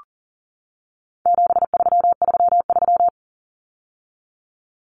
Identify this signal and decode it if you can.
Morse code, heard as a keyed tone, about 40 wpm, 700 Hz. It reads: 7333
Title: Numbers - 4-Digit 40wpm